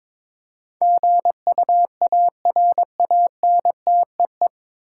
Morse code, heard as a keyed tone, about 21 wpm, 700 Hz. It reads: GUARANTEE
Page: Morse Code Ninja practice 22wpm